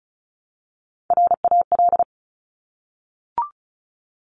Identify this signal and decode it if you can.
Morse code, heard as a keyed tone, about 36 wpm, 700 Hz. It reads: RAL
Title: Top Three Letter Combinations in Words 35wpm